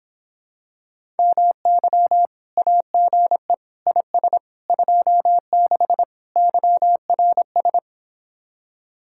Morse code, heard as a keyed tone, about 26 wpm, 700 Hz. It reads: MY AGE IS 26 YRS